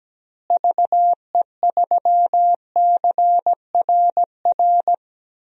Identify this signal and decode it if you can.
Morse code, heard as a keyed tone, about 17 wpm, 700 Hz. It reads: VE3CRR
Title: DXCC-100 Call Signs - 6 Characters - Repeated 3x - 17wpm